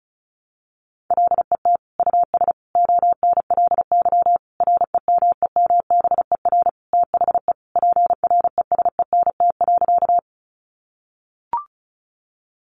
Morse code, heard as a keyed tone, about 34 wpm, 700 Hz. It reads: LET US ONLY REMEMBER THE PRESENT.
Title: Sentences from Top 800 Words 35wpm